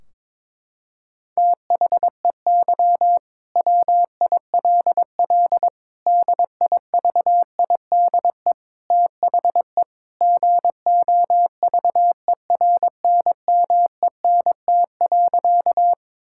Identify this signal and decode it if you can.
Morse code, heard as a keyed tone, about 22 wpm, 700 Hz. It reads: THEY WILL DIVIDE THE GOVERNMENT.
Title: Sentences from Top 1600 Words 22wpm